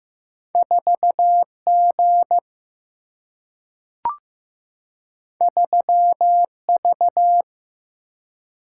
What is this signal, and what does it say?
Morse code, heard as a keyed tone, about 15 wpm, 700 Hz. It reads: 4G 3V
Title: Single-Letter Single-Number 15wpm